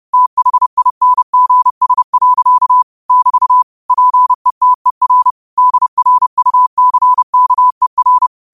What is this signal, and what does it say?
Morse code, heard as a keyed tone, about 30 wpm, 1000 Hz. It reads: THINGS. <BT> PETER DRUCKER
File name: qotdfiles / 30wpm